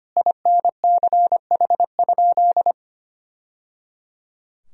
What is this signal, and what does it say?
Morse code, heard as a keyed tone, about 25 wpm, 700 Hz. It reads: INCH?
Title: Sentences from Top 1000 Words - Review Part 3 - 25wpm